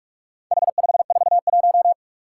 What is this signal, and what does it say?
Morse code, heard as a keyed tone, about 46 wpm, 700 Hz. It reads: H541
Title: Four Challenging Letter-Number Combinations 45wpm